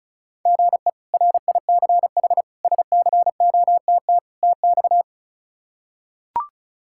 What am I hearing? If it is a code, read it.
Morse code, heard as a keyed tone, about 34 wpm, 700 Hz. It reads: GE RICH SCOTT TX
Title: K1USN SST QSO Practice - 35wpm